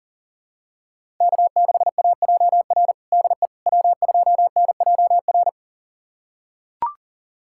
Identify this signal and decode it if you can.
Morse code, heard as a keyed tone, about 40 wpm, 700 Hz. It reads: K6AJR DE W2NJR
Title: USA Callsign DE Callsign - 5 Characters - 40wpm